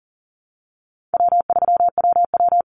Morse code, heard as a keyed tone, about 40 wpm, 700 Hz. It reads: W3WW
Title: Easy Call Signs - 40wpm